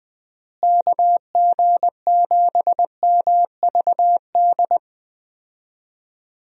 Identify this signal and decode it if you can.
Morse code, heard as a keyed tone, about 20 wpm, 700 Hz. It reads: KG7MVD
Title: USA Call Signs - 6 Characters - Repeated 2x - 20wpm